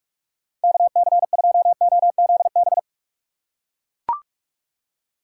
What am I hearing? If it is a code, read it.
Morse code, heard as a keyed tone, about 41 wpm, 700 Hz. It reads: KC2OZB